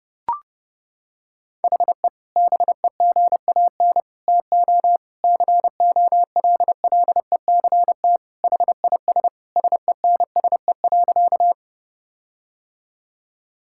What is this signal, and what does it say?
Morse code, heard as a keyed tone, about 32 wpm, 700 Hz. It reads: HE BEGAN TO COLLECT HIS SENSE.
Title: Sentences from Top 1600 Words 30wpm